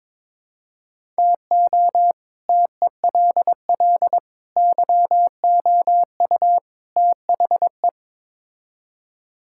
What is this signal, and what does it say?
Morse code, heard as a keyed tone, about 22 wpm, 700 Hz. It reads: TO TELL YOU THE
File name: practice / 22wpm / Build A Phrase - 5-Word Phrases - 22wpm